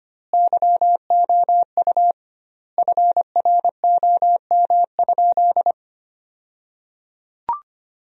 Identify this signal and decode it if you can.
Morse code, heard as a keyed tone, about 25 wpm, 700 Hz. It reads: YOU FROM?
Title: 2x Word-Spacing - Sentences from Top 200 Words - 25wpm